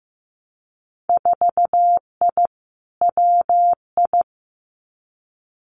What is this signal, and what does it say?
Morse code, heard as a keyed tone, about 15 wpm, 700 Hz. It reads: 4I WI
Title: Winter Field Day Exchanges 15wpm